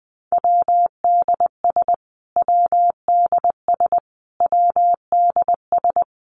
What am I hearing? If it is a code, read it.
Morse code, heard as a keyed tone, about 20 wpm, 700 Hz. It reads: WDS WDS WDS
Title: Abbreviations Repeated 3x - 20wpm